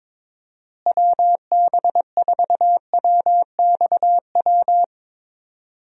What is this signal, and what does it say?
Morse code, heard as a keyed tone, about 21 wpm, 700 Hz. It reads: WB4WXW
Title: USA Call Signs - 6 Characters - Repeated 2x - 22wpm